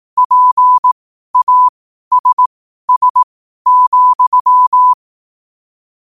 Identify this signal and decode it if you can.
Morse code, heard as a keyed tone, about 18 wpm, spaced as for 12 wpm, 1000 Hz. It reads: PASS,
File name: qotdfiles / 12wpm